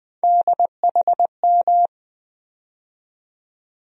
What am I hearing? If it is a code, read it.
Morse code, heard as a keyed tone, about 20 wpm, 700 Hz. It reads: DHM
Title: USA Call Sign Postfixes - 3 Letters 20wpm